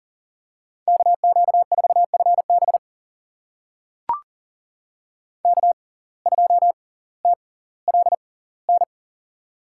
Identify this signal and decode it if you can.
Morse code, heard as a keyed tone, about 38 wpm, spaced as for 20 wpm, 700 Hz. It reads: KQ4FB K2TLN
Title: USA Call Signs - 5 Characters - 6x Character Spacing - 40wpm